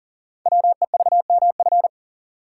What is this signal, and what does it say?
Morse code, heard as a keyed tone, about 42 wpm, 690 Hz. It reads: WEVMF
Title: Code Groups - 5 Letters - 40wpm